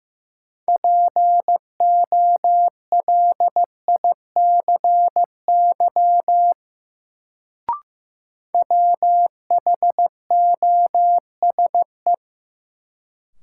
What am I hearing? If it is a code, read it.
Morse code, heard as a keyed tone, about 16 wpm, 700 Hz. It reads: POLICY WHOSE
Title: Sentences from Top 1200 Words - New Word Drill Alt - 15wpm